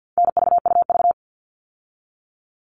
Morse code, heard as a keyed tone, about 50 wpm, 700 Hz. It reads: NVUV
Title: All Cut Numbers - 4-Digit 50wpm